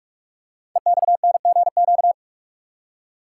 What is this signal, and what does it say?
Morse code, heard as a keyed tone, about 44 wpm, 690 Hz. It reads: EXNGQ
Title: Code Groups - 5 Letters - 45wpm